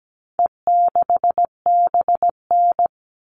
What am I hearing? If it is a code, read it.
Morse code, heard as a keyed tone, about 17 wpm, 700 Hz. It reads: E6BN
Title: All Cut Numbers - 4-Digit 17wpm